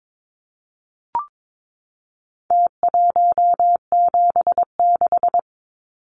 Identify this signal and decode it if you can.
Morse code, heard as a keyed tone, about 22 wpm, 700 Hz. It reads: T176
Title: Numbers - Contest Style - 4-Digit 22wpm